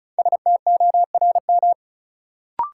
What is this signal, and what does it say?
Morse code, heard as a keyed tone, about 36 wpm, 700 Hz. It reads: STORM
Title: Sentences from Top 1500 Words - New Word Drill Alt - 35wpm